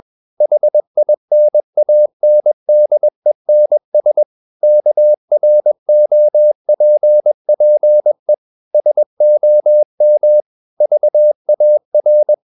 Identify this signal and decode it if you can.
Morse code, heard as a keyed tone, about 21 wpm, 600 Hz. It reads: HINANDENS KROPPE SOM VAR
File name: Ferie QRS 21wpm